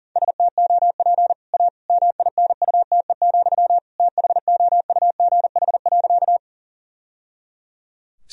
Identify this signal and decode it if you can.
Morse code, heard as a keyed tone, about 40 wpm, 700 Hz. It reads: STOP A MINUTE, THOUGH.
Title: Sentences from Top 1000 Words - Review Part 2 - 40wpm